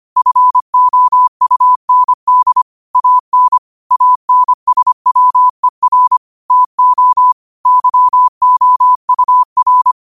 Morse code, heard as a keyed tone, about 25 wpm, 1000 Hz. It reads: FOUND AN ANSWER TO YOUR
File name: qotdfiles / 25wpm